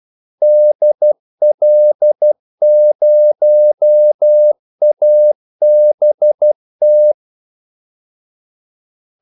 Morse code, heard as a keyed tone, about 12 wpm, 600 Hz. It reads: DL0ABT
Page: 2 ekstra ordmellemrum mellem kaldesignalerne